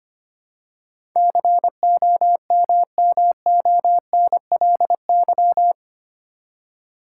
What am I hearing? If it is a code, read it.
Morse code, heard as a keyed tone, about 25 wpm, 700 Hz. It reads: COMMONLY
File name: Eight Letter Words - 25wpm